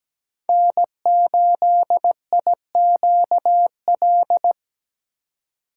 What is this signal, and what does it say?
Morse code, heard as a keyed tone, about 17 wpm, 700 Hz. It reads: N8IQL
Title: USA Call Signs - 5 Characters - 2x Character Spacing - 17wpm